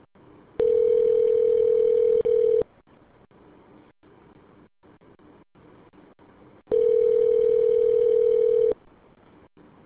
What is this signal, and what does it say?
Ringback tone (North America), 440 + 480 Hz.